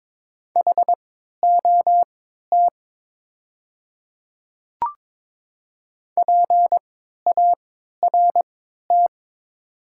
Morse code, heard as a keyed tone, about 22 wpm, spaced as for 12 wpm, 700 Hz. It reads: HOT PART
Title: 3x Character Spacing - Top 200 Words - 22wpm